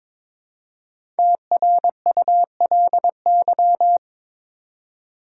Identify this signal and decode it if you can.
Morse code, heard as a keyed tone, about 22 wpm, 700 Hz. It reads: TRULY
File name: 5 Letter Words - First 3 Letters Given - 22wpm